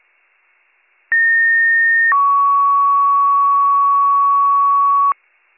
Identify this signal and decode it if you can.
Two-tone page, heard as a one-second tone, then a three-second tone, about 1850 then 1120 Hz. It is heard from 1.1 to 5.1 s.